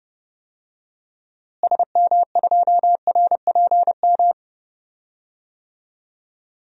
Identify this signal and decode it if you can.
Morse code, heard as a keyed tone, about 32 wpm, 700 Hz. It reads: SM2RPM
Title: DXCC-100 Call Signs - 6 Characters - Repeated 3x - 30wpm